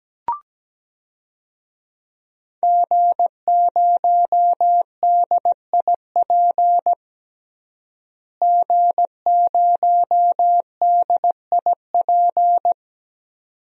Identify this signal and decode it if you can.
Morse code, heard as a keyed tone, about 17 wpm, 700 Hz. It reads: G0DIP G0DIP
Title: DXCC-100 Call Signs - 5 Characters - Repeated 2x - 17wpm